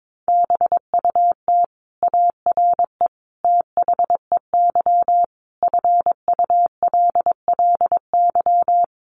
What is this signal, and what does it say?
Morse code, heard as a keyed tone, about 22 wpm, 700 Hz. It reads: BUT ARE THEY FULLY
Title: Sentences from Top 1300 Words 22wpm